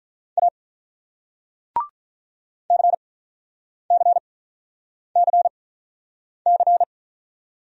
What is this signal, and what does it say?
Morse code, heard as a keyed tone, about 46 wpm, spaced as for 9 wpm, 700 Hz. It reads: A CCCC